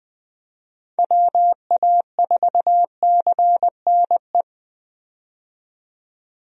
Morse code, heard as a keyed tone, about 21 wpm, 700 Hz. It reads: WA4CNE
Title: USA Call Signs - 6 Characters - 20wpm